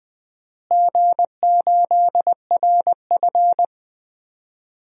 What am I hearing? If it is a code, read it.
Morse code, heard as a keyed tone, about 20 wpm, 700 Hz. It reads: G8RF
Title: DXCC-100 Call Signs - 4 Characters - Repeated 3x - 20wpm